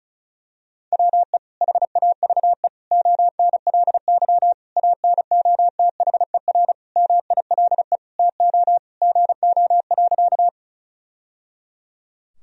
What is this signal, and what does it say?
Morse code, heard as a keyed tone, about 36 wpm, 700 Hz. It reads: WE HAVE ONLY ANOTHER MILE TO GO.
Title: Sentences from Top 1000 Words - Review Part 1 - 35wpm